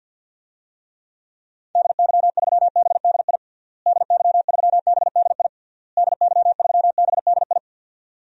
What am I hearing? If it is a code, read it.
Morse code, heard as a keyed tone, about 52 wpm, 700 Hz. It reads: DY3BDI DY3BDI DY3BDI
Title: DXCC-100 Call Signs - 6 Characters - Repeated 3x - 50wpm